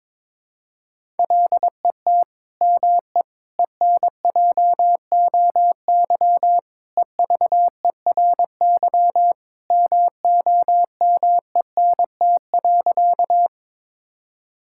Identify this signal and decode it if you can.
Morse code, heard as a keyed tone, about 22 wpm, 700 Hz. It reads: LET ME ENJOY EVERY MOMENT.
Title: Sentences from Top 1000 Words - Review Part 5 - 22wpm